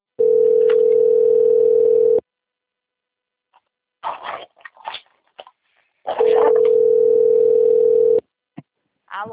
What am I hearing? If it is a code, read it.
Ringback tone (North America), 440 + 480 Hz.